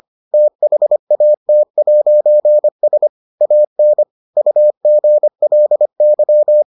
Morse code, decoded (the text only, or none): THAT'S AN UGLY